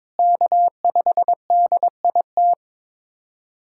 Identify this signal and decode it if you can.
Morse code, heard as a keyed tone, about 22 wpm, 700 Hz. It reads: K5DIT
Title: USA Callsign DE Callsign - 5 Characters - 22wpm